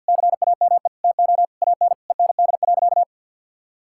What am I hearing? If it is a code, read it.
Morse code, heard as a keyed tone, about 50 wpm, 700 Hz. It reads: CAME TO AN END.
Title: Sentences from Top 1600 Words 50wpm